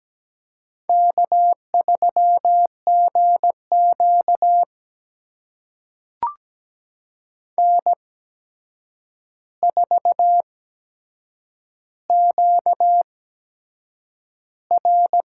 Morse code, heard as a keyed tone, about 17 wpm, spaced as for 7 wpm, 700 Hz. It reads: K3GQ N4QR
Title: USA Call Signs - 4 Characters - 8x Character Spacing - 17wpm